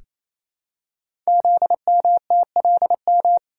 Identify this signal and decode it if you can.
Morse code, heard as a keyed tone, about 29 wpm, 700 Hz. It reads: ZMTLM
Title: Code Groups - 5 Letters - 28wpm